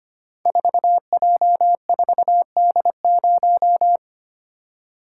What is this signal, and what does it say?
Morse code, heard as a keyed tone, about 25 wpm, 700 Hz. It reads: 4J4D0
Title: Code Groups - 5 Letter:Numbers - 25wpm